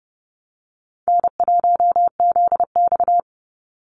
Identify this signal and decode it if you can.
Morse code, heard as a keyed tone, about 28 wpm, 700 Hz. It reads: N1ZX
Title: Easy Call Signs Repeated 3x - 30wpm